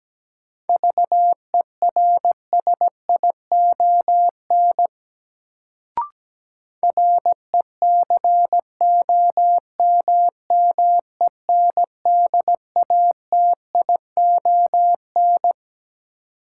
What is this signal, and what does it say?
Morse code, heard as a keyed tone, about 17 wpm, 700 Hz. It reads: VERSION RECOMMENDATION
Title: Sentences from Top 1400 Words - New Word Drill Alt - 17wpm